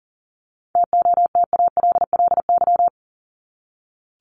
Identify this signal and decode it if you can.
Morse code, heard as a keyed tone, about 40 wpm, 700 Hz. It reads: TOTALLY
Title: LY Suffix - Top 100 Words with Base Words 40wpm